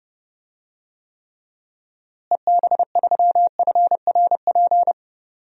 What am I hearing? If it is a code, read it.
Morse code, heard as a keyed tone, about 32 wpm, 700 Hz. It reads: EB3FRP
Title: DXCC-100 Call Signs - 6 Characters - Repeated 2x - 30wpm